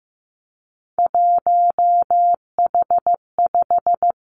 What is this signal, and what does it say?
Morse code, heard as a keyed tone, about 15 wpm, 700 Hz. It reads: 1H5
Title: Three Challenging Letter-Number Combinations 15wpm